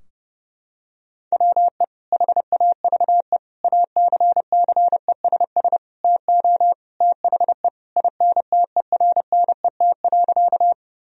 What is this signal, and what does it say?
Morse code, heard as a keyed tone, about 30 wpm, 700 Hz. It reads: WE HAVE ACCESS TO THE INTERNET.
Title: Sentences from Top 1000 Words - Review Part 4 - 30wpm